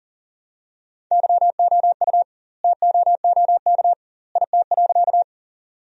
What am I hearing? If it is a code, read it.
Morse code, heard as a keyed tone, about 40 wpm, 700 Hz. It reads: YOU TOOK IT.
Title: 2x Word-Spacing - Sentences from Top 300 Words - 40wpm